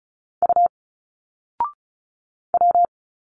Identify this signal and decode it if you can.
Morse code, heard as a keyed tone, about 34 wpm, 700 Hz. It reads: U W
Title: Single Letters 35wpm